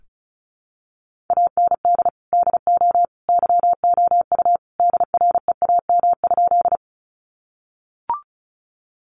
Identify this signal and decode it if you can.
Morse code, heard as a keyed tone, about 34 wpm, 700 Hz. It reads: AND DO YOU DREAM?
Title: Sentences from Top 1500 Words 35wpm